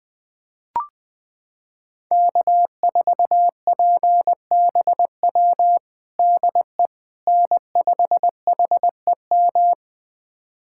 Morse code, heard as a keyed tone, about 20 wpm, 700 Hz. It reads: K4PBW DE N5HEM
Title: USA Callsign DE Callsign - 5 Characters - 20wpm